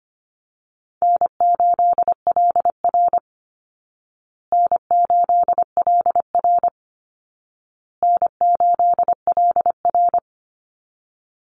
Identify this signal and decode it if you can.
Morse code, heard as a keyed tone, about 25 wpm, 700 Hz. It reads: N8LR N8LR N8LR
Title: Easy Call Signs Repeated 3x - 25wpm